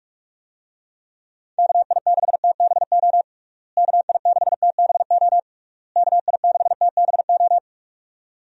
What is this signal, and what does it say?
Morse code, heard as a keyed tone, about 48 wpm, 700 Hz. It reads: KI6TBO KI6TBO KI6TBO
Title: USA Call Signs - 6 Characters - Repeated 3x - 45wpm